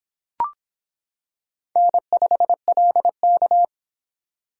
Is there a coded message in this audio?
Yes: Morse code from 1.8 to 3.6 s, heard as a keyed tone, about 26 wpm, 700 Hz. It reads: N5LK